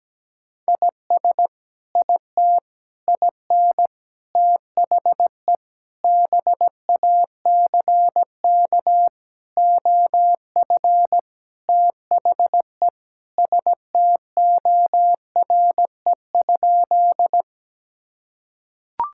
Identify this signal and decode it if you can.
Morse code, heard as a keyed tone, about 17 wpm, 700 Hz. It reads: IS IT IN THE BACK OF THE STORE?
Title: Sentences from Top 1000 Words - Review Part 5 - 17wpm